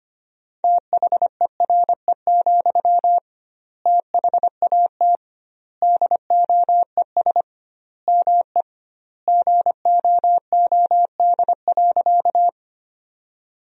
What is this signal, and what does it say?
Morse code, heard as a keyed tone, about 25 wpm, 700 Hz. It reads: THERE, THAT DOES ME GOOD.
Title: 2x Word-Spacing - Sentences from Top 200 Words - 25wpm